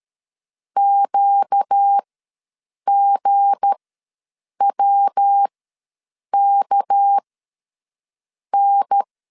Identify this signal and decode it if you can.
Morse code, heard as a keyed tone, about 13 wpm, spaced as for 6 wpm, 800 Hz. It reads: QGWKN
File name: LEZIONI / 7 GF9Z / 7 G ws 12wpm 3140